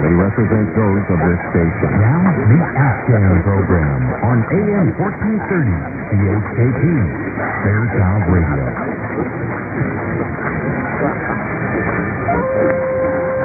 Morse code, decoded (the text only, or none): none